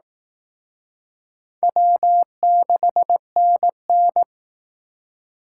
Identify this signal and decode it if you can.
Morse code, heard as a keyed tone, about 18 wpm, 700 Hz. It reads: W6NN
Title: Speed Racing - USA Call Signs 15wpm